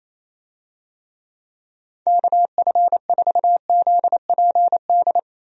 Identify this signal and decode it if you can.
Morse code, heard as a keyed tone, about 28 wpm, 700 Hz. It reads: KF4ZPD
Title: USA Call Signs - 6 Characters - 28wpm